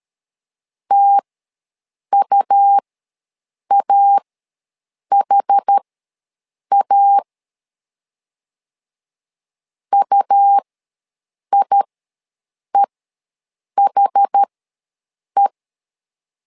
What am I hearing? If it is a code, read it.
Morse code, heard as a keyed tone, about 13 wpm, spaced as for 7 wpm, 800 Hz. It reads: TUAHA UIEHE
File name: LEZIONI / 2 UHDN / 2 UH ws 12wpm 1730